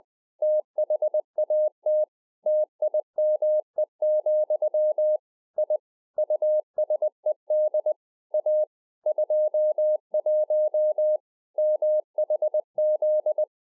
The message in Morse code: THAT TIME, I USED A 21 MHZ

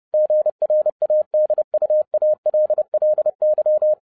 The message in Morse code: GRADUALLY

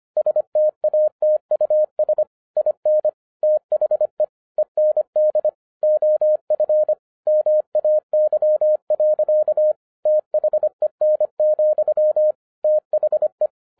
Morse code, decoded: STATUS IN THE END OF MAY. THEN, THE

